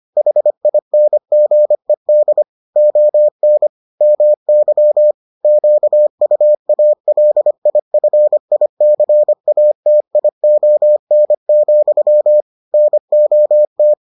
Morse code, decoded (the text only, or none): HINGED ON MY QUALIFICATION, NOT